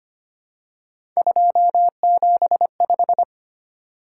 Morse code, heard as a keyed tone, about 25 wpm, 700 Hz. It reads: 275
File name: Three Letter-Number Combinations - 25wpm